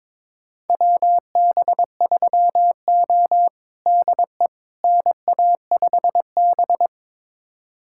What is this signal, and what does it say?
Morse code, heard as a keyed tone, about 22 wpm, 700 Hz. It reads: WB3O DE NA5B
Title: USA Callsign DE Callsign - 4 Characters - 22wpm